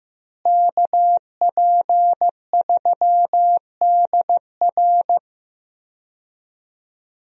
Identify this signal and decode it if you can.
Morse code, heard as a keyed tone, about 15 wpm, 700 Hz. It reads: KP3DR
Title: USA Call Signs - 5 Characters - Repeated 2x - 15wpm